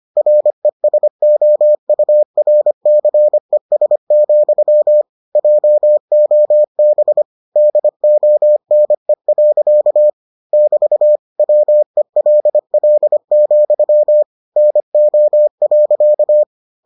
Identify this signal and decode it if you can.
Morse code, heard as a keyed tone, about 25 wpm, 600 Hz. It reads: RESOURCES, JOB DONE. <BT> WELL, NO.